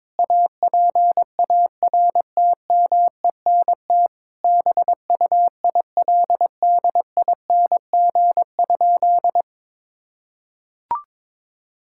Morse code, heard as a keyed tone, about 22 wpm, 700 Hz. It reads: APARTMENT BUILDING?
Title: Sentences from Top 1000 Words - Review Part 6 - 22wpm